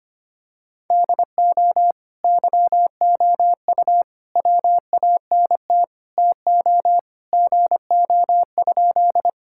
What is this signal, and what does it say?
Morse code, heard as a keyed tone, about 25 wpm, 700 Hz. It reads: DO YOU WANT TO GO?
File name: practice / 25wpm / Sentences with 3-Word Phrases - 25wpm